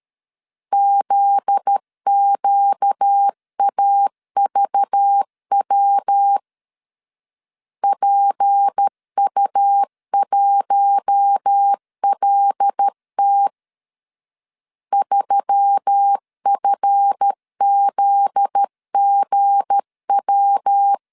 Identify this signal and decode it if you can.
Morse code, heard as a keyed tone, about 13 wpm, 800 Hz. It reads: ZQAVW PU1LT 3FZGW
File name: LEZIONI / 7 GF9Z / 7 GF9Z 12wpm 2147